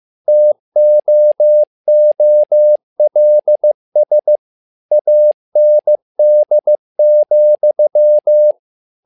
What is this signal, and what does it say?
Morse code, heard as a keyed tone, about 15 wpm, 600 Hz. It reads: TOOLS AND,
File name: learning center / tower trouble / tower trouble 15wpm